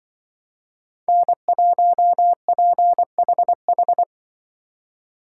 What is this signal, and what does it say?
Morse code, heard as a keyed tone, about 24 wpm, 700 Hz. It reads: N1PHH
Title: Speed Racing - USA Call Signs 20wpm